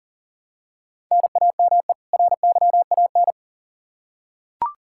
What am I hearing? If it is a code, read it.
Morse code, heard as a keyed tone, about 40 wpm, 700 Hz. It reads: NAME RYAN
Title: K1USN SST QSO Practice 40wpm